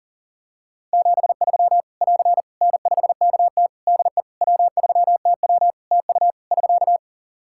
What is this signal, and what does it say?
Morse code, heard as a keyed tone, about 40 wpm, 700 Hz. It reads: 73 <AR> N5KT DE W3TW TU <SK>
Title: Practice QSO - Exchange 4 - Ending - 40wpm